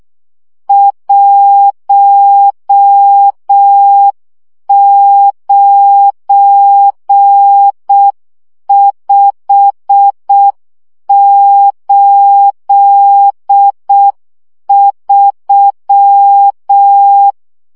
Morse code, heard as a keyed tone, about 6 wpm, 800 Hz. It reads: 19583
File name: morze vizsga 2011-2 5